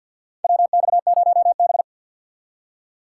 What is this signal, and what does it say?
Morse code, heard as a keyed tone, about 52 wpm, 700 Hz. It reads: WX0B
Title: USA Call Signs - 4 Characters - 3x Character Spacing - 50wpm